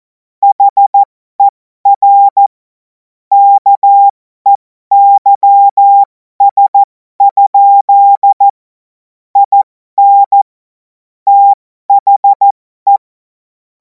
Morse code, heard as a keyed tone, about 14 wpm, spaced as for 12 wpm, 800 Hz. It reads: HER KEYS? IN THE
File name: Morsekurs / Witze